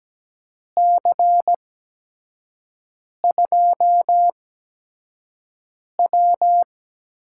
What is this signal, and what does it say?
Morse code, heard as a keyed tone, about 17 wpm, 700 Hz. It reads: C 2 W
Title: USA Call Signs - 8x Character Spacing - 17wpm